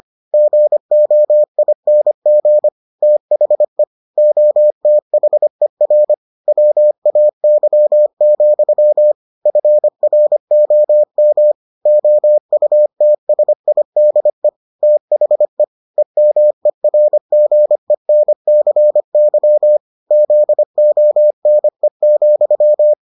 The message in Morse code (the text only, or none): GOING THE OTHER WAY, FROM OUTSIDE THE EMERGENCY ZONE,